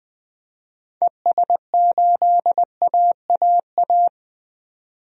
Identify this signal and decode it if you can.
Morse code, heard as a keyed tone, about 20 wpm, 700 Hz. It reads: ES8AAA
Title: DXCC-100 Call Signs - 6 Characters - 20wpm